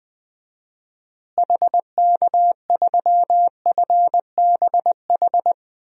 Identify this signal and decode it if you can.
Morse code, heard as a keyed tone, about 21 wpm, 700 Hz. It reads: HK3FBH
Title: DXCC-100 Call Signs - 6 Characters - Repeated 2x - 20wpm